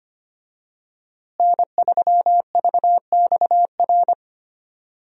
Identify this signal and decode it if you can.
Morse code, heard as a keyed tone, about 25 wpm, 700 Hz. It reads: N3VXR
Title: USA Call Signs - 5 Characters - Repeated 3x - 25wpm